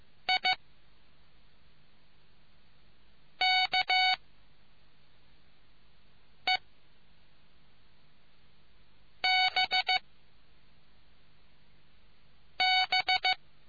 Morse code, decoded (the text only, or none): IKEBB